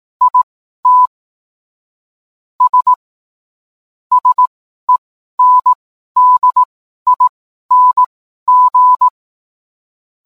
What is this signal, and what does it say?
Morse code, heard as a keyed tone, about 18 wpm, spaced as for 12 wpm, 1000 Hz. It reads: IT S SENDING